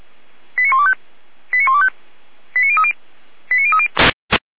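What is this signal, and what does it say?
Five-tone selective call (ZVEI), heard as quick runs of five tones, tones of about 70 ms. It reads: 89125 89125 89030 89030
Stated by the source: ZVEI five-tone selective call received by SDR.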